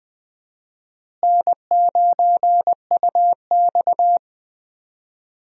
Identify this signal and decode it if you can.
Morse code, heard as a keyed tone, about 20 wpm, 700 Hz. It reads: N9UX
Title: USA Call Signs - 4 Characters - Repeated 3x - 20wpm